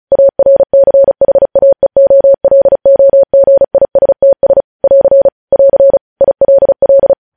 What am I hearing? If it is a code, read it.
Morse code, heard as a keyed tone, about 36 wpm, 550 Hz. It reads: ARCHAEOLOGISTS <AR> <AR> ILL
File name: podcast / News headlines-1x-35x35